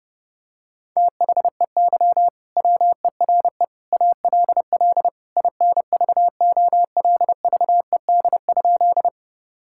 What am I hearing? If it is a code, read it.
Morse code, heard as a keyed tone, about 30 wpm, 700 Hz. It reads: THEY WERE ALL INVOLVED?
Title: Sentences from Top 1000 Words - Review Part 5 - 30wpm